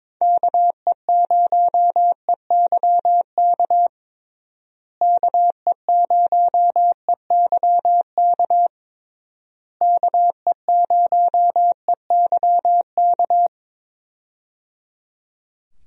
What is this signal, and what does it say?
Morse code, heard as a keyed tone, about 22 wpm, 700 Hz. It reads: KE0EYK KE0EYK KE0EYK
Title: USA Call Signs - 6 Characters - Repeated 3x - 22wpm